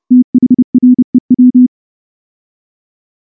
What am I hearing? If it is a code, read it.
Morse code, heard as a keyed tone, about 28 wpm, 250 Hz. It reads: THREW
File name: morse / morse-training-30WPM